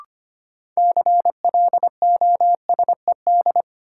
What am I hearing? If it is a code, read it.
Morse code, heard as a keyed tone, about 25 wpm, 700 Hz. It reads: CLOSED
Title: Sentences from Top 1400 Words - New Word Drill Alt - 25wpm